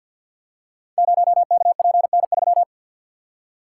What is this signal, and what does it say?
Morse code, heard as a keyed tone, about 53 wpm, 700 Hz. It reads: 0KPN3
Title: Code Groups - 5 Letter:Numbers - 50wpm